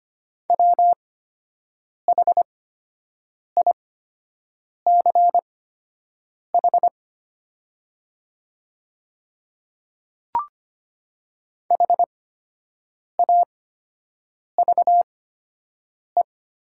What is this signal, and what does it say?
Morse code, heard as a keyed tone, about 25 wpm, spaced as for 7 wpm, 700 Hz. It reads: WHICH HAVE
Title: 8x Character Spacing - Top 100 Words - 25wpm